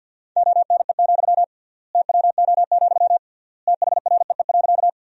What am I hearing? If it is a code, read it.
Morse code, heard as a keyed tone, about 52 wpm, 700 Hz. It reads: ONE, TWO, THREE.
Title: 3x Word-Spacing - Sentences from Top 200 Words - 50wpm